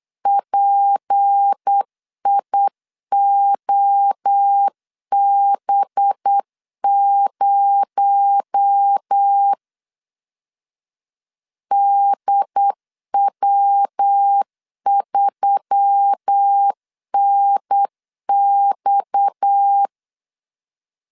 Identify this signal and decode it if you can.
Morse code, heard as a keyed tone, about 8 wpm, 800 Hz. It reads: PIOB0 DW3NX
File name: 8wpm